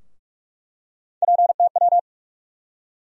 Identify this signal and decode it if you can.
Morse code, heard as a keyed tone, about 45 wpm, 700 Hz. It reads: PTW
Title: USA Call Sign Postfixes - 3 Letters 45wpm